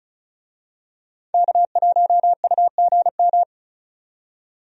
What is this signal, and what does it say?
Morse code, heard as a keyed tone, about 34 wpm, 700 Hz. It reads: K1UGM
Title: USA Call Signs - 5 Characters - 35wpm